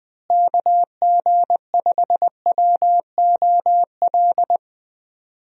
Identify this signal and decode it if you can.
Morse code, heard as a keyed tone, about 20 wpm, 700 Hz. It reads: KG5WOL
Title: USA Call Signs - 6 Characters - Repeated 3x - 20wpm